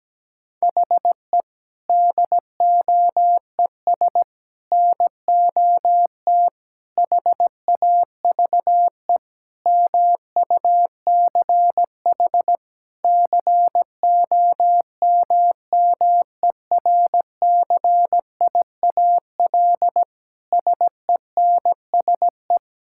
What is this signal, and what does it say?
Morse code, heard as a keyed tone, about 17 wpm, 700 Hz. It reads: HE DOES NOT HAVE MUCH COMMERCIAL SENSE
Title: Sentences from Top 1000 Words - Review Part 5 - 17wpm